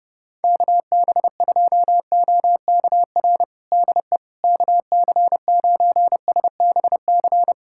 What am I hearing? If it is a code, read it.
Morse code, heard as a keyed tone, about 32 wpm, 700 Hz. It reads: KB2OKR DE KC9SBC
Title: USA Callsign DE Callsign - 6 Characters - 30wpm